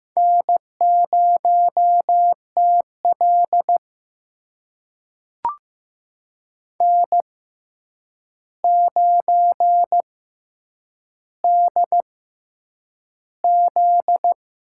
Morse code, heard as a keyed tone, about 15 wpm, spaced as for 8 wpm, 700 Hz. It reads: N0TL N9DZ
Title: USA Call Signs - 4 Characters - 6x Character Spacing - 15wpm